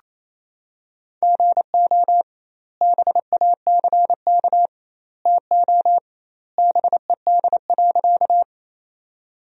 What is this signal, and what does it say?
Morse code, heard as a keyed tone, about 28 wpm, 700 Hz. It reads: GO BACK TO BED.
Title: 2x Word-Spacing - Sentences from Top 500 Words - 28wpm